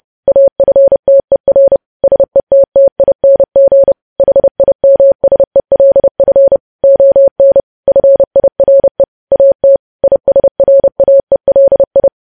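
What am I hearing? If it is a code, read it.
Morse code, heard as a keyed tone, about 30 wpm, 550 Hz. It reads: AFTER SETTING HIMSELF ON FIRE AT ISRAELI
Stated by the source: News headlines sent in Morse at 30 WPM.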